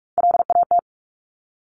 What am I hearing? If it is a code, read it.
Morse code, heard as a keyed tone, about 46 wpm, 700 Hz. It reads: LAT
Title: Top Three Letter Combinations in Words 45wpm